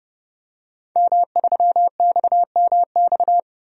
Morse code, heard as a keyed tone, about 32 wpm, 700 Hz. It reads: M3XMX
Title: DXCC-100 Call Signs - 5 Characters - 30wpm